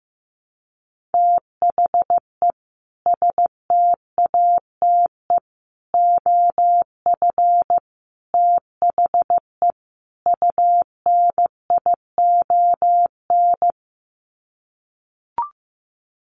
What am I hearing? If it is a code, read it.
Morse code, heard as a keyed tone, about 15 wpm, 700 Hz. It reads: THE STATE OF THE UNION
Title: Sets of 5 Words Encore 15wpm